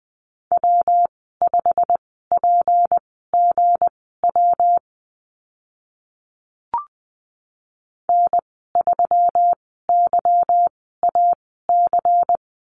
Morse code, heard as a keyed tone, about 20 wpm, spaced as for 15 wpm, 700 Hz. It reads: W5PGW N3YAC
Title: 167 Lesson 27g - Number 7 - Call Signs 20wpm